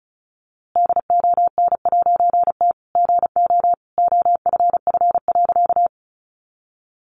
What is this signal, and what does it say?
Morse code, heard as a keyed tone, about 34 wpm, 700 Hz. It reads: DON'T GO OFF.
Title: A Sound Of Thunder 35wpm